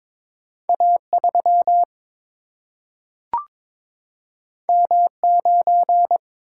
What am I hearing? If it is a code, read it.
Morse code, heard as a keyed tone, about 22 wpm, 700 Hz. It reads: A3 M9
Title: Single-Letter Single-Number 22wpm